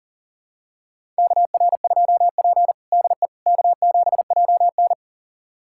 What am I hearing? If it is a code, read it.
Morse code, heard as a keyed tone, about 40 wpm, 700 Hz. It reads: KR2P DE K7JN
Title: USA Callsign DE Callsign - 4 Characters - 40wpm